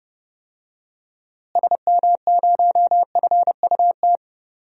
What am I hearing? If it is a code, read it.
Morse code, heard as a keyed tone, about 30 wpm, 700 Hz. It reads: SM0FUT
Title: DXCC-100 Call Signs - 6 Characters - Repeated 2x - 30wpm